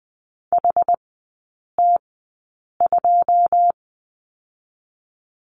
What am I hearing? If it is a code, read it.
Morse code, heard as a keyed tone, about 20 wpm, spaced as for 8 wpm, 700 Hz. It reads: HT2
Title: 100 Lesson 17e - Letter u - Three Characters 20wpm